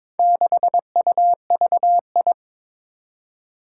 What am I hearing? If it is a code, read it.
Morse code, heard as a keyed tone, about 22 wpm, 700 Hz. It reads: 6UVI
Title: Four Challenging Letter-Number Combinations 22wpm